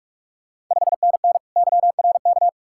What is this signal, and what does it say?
Morse code, heard as a keyed tone, about 46 wpm, 700 Hz. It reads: 5NN YRK
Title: Ontario QSO Party Exchange 45wpm